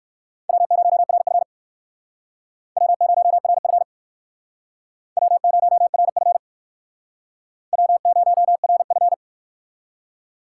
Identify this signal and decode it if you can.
Morse code, heard as a keyed tone, about 57 wpm, 700 Hz. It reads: W0RF W0RF W0RF W0RF